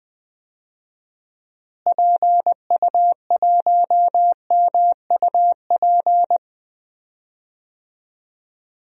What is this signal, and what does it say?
Morse code, heard as a keyed tone, about 21 wpm, 700 Hz. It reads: PU1MUP
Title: DXCC-100 Call Signs - 6 Characters - Repeated 2x - 20wpm